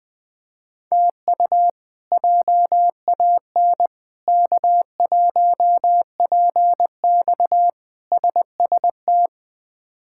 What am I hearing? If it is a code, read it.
Morse code, heard as a keyed tone, about 20 wpm, 700 Hz. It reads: TU JAN K1PX SST
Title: K1USN SST QSO Practice - 20wpm